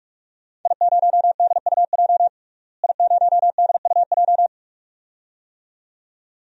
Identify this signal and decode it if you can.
Morse code, heard as a keyed tone, about 44 wpm, 700 Hz. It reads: I0DUJ I0DUJ
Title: DXCC-100 Call Signs - 5 Characters - Repeated 2x - 45wpm